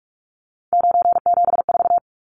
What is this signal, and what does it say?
Morse code, heard as a keyed tone, about 44 wpm, 700 Hz. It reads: 974
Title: Numbers - 3-Digit 45wpm